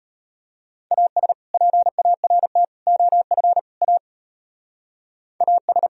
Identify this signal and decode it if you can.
Morse code, heard as a keyed tone, about 31 wpm, 700 Hz. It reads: AS PART OF A AS